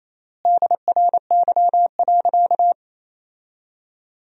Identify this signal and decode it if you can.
Morse code, heard as a keyed tone, about 28 wpm, 700 Hz. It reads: DRY.
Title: 3x Word-Spacing - Sentences from Top 500 Words - 28wpm